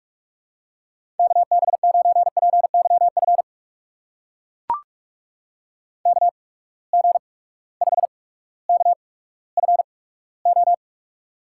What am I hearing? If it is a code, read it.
Morse code, heard as a keyed tone, about 44 wpm, spaced as for 19 wpm, 700 Hz. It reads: KB9PYF KG5KFO
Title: USA Call Signs - 6 Characters - 8x Character Spacing - 45wpm